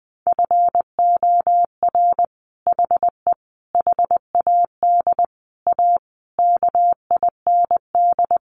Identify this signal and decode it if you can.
Morse code, heard as a keyed tone, about 20 wpm, 700 Hz. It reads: FOR HE HAD A KIND
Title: Sentences from Top 1400 Words 20wpm